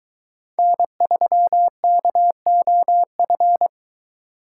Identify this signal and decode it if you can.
Morse code, heard as a keyed tone, about 22 wpm, 700 Hz. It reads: N3KOF